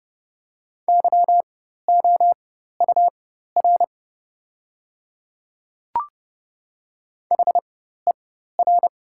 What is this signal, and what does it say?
Morse code, heard as a keyed tone, about 29 wpm, spaced as for 14 wpm, 690 Hz. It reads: YOUR HER
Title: 4x Character Spacing - Top 100 Words - 30wpm